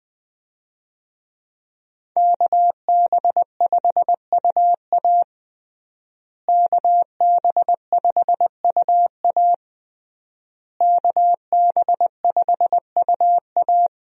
Morse code, heard as a keyed tone, about 20 wpm, 700 Hz. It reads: KB5UA KB5UA KB5UA
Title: USA Call Signs - 5 Characters - Repeated 3x - 20wpm